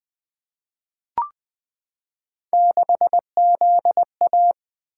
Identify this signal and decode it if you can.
Morse code, heard as a keyed tone, about 20 wpm, 700 Hz. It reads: 6ZA
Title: Three Letter-Number Combinations - 20wpm